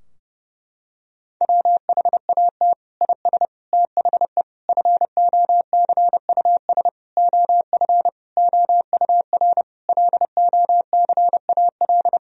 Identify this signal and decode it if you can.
Morse code, heard as a keyed tone, about 32 wpm, 700 Hz. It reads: WHAT IS THE FOCUS OF OUR LOCAL
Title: Sentences from Top 1000 Words - Review Part 4 - 30wpm